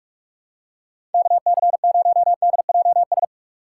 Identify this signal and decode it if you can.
Morse code, heard as a keyed tone, about 46 wpm, 700 Hz. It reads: KC0DJS
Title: USA Call Signs - 6 Characters - 2x Character Spacing - 45wpm